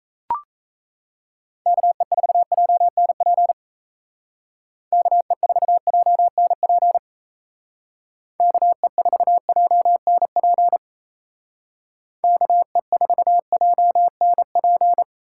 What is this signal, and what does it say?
Morse code, heard as a keyed tone, about 36 wpm, 700 Hz. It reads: KE4JNP KE4JNP KE4JNP KE4JNP